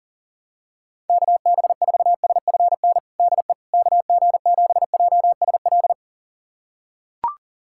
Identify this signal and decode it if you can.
Morse code, heard as a keyed tone, about 40 wpm, 700 Hz. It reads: KB4SFN DE KG7JSL
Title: USA Callsign DE Callsign - 6 Characters - 40wpm